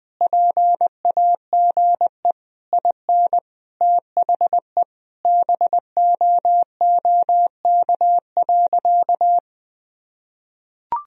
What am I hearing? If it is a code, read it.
Morse code, heard as a keyed tone, about 20 wpm, 700 Hz. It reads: PAGE IN THE BOOK.
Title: Sentences from Top 1000 Words - Review Part 1 - 20wpm